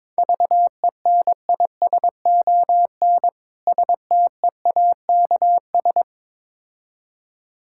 Morse code, heard as a keyed tone, about 22 wpm, 690 Hz. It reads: VENISON STEAKS
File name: Difficult Words with Hints - 22wpm